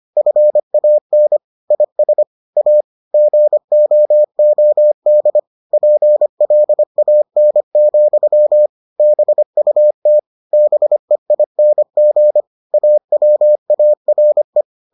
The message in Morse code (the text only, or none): FAN IS A GOOD PLAN, BUT BEING AWARE